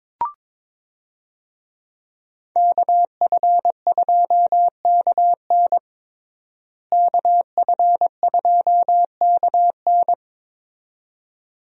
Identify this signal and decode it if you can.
Morse code, heard as a keyed tone, about 22 wpm, 700 Hz. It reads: KF2KN KF2KN
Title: USA Call Signs - 5 Characters - Repeated 2x - 22wpm